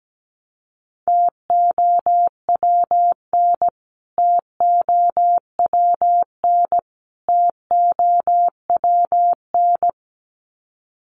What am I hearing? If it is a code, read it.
Morse code, heard as a keyed tone, about 17 wpm, 700 Hz. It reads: TOWN TOWN TOWN
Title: Top 500 Words Repeated 17wpm